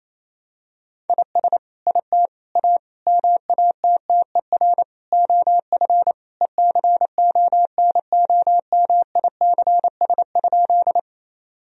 Morse code, heard as a keyed tone, about 27 wpm, 700 Hz. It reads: IS IT A MATTER OF ECONOMICS?
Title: Sentences from Top 1000 Words - Review Part 3 - 28wpm